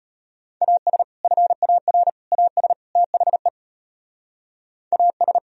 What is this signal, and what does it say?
Morse code, heard as a keyed tone, about 33 wpm, 700 Hz. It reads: AS FAR AS THE AS